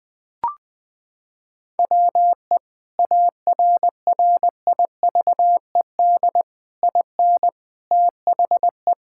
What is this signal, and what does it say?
Morse code, heard as a keyed tone, about 20 wpm, 700 Hz. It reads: WE ARRIVED IN THE
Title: Sentences with 8-Word Phrases - 20wpm